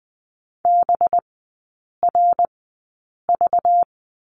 Morse code, heard as a keyed tone, about 20 wpm, spaced as for 8 wpm, 700 Hz. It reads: BRV
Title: 191 Lesson 31e - Letter v - Three Characters 20wpm